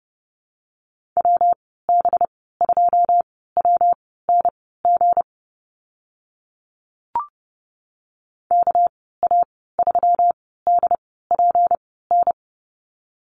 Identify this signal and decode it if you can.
Morse code, heard as a keyed tone, about 30 wpm, spaced as for 17 wpm, 700 Hz. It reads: WB2WNG KA3DPN
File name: Lesson 37g - Letter x - Call Signs 30wpm